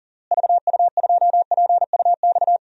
Morse code, heard as a keyed tone, about 40 wpm, 700 Hz. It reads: VU2PUX
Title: DXCC-100 Call Signs - 6 Characters - Repeated 3x - 40wpm